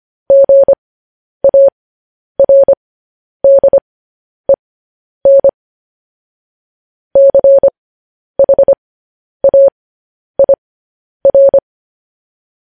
Morse code, heard as a keyed tone, about 25 wpm, spaced as for 10 wpm, 550 Hz. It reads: GARDEN CHAIR